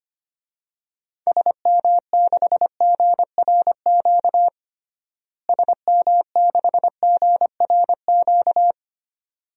Morse code, heard as a keyed tone, about 25 wpm, 700 Hz. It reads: SM6GRQ SM6GRQ
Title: DXCC-100 Call Signs - 6 Characters - Repeated 2x - 25wpm